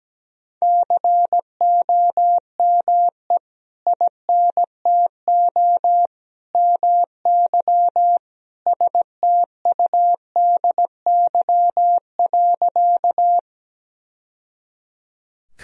Morse code, heard as a keyed tone, about 17 wpm, 700 Hz. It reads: COME INTO MY STUDY.
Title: Sentences from Top 1000 Words - Review Part 1 - 17wpm